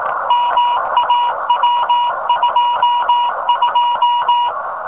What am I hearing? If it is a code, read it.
Morse code, heard as a keyed tone, about 18 wpm, 1000 Hz, with noise. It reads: MAW22
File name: rs15 telem